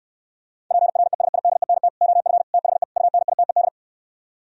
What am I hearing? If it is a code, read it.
Morse code, heard as a keyed tone, about 67 wpm, 700 Hz. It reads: PRESIDENT OF THE UNITED